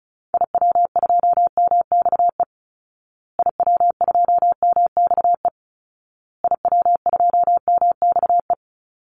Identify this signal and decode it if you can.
Morse code, heard as a keyed tone, about 34 wpm, 700 Hz. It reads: IW2MXE IW2MXE IW2MXE
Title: Call Signs Repeated 2x - 35wpm